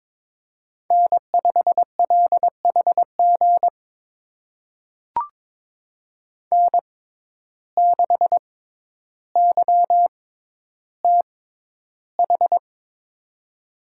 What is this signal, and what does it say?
Morse code, heard as a keyed tone, about 21 wpm, spaced as for 11 wpm, 700 Hz. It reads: N5LHG N6YTH
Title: USA Call Signs - 5 Characters - 6x Character Spacing - 22wpm